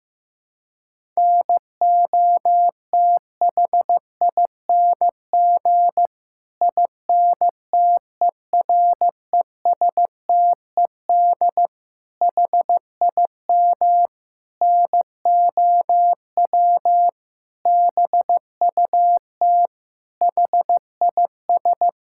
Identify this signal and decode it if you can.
Morse code, heard as a keyed tone, about 15 wpm, 700 Hz. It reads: NOTHING INTERESTED HIM NOW BUT HIS
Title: Sentences from Top 1000 Words - Review Part 6 - 15wpm